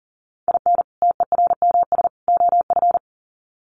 Morse code, heard as a keyed tone, about 40 wpm, 700 Hz. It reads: IN TERMS OF
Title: Sets of 3 Words 40wpm